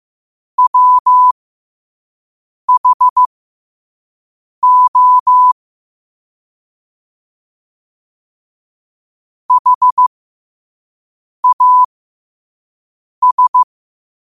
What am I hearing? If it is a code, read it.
Morse code, heard as a keyed tone, about 15 wpm, spaced as for 5 wpm, 1000 Hz. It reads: WHO HAS
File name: qotdfiles / 5wpm